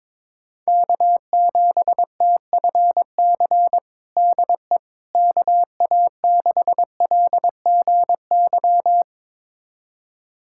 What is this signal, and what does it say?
Morse code, heard as a keyed tone, about 22 wpm, 700 Hz. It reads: K7TFC DE KA6LGY
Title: USA Callsign DE Callsign 22wpm